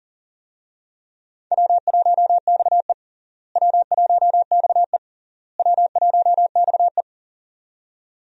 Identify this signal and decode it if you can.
Morse code, heard as a keyed tone, about 40 wpm, 700 Hz. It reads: W1XE W1XE W1XE
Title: USA Call Signs - 4 Characters - Repeated 3x - 40wpm